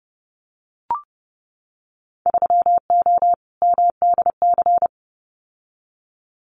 Morse code, heard as a keyed tone, about 30 wpm, 700 Hz. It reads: 3O MDC
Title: Winter Field Day Exchanges Encore 30wpm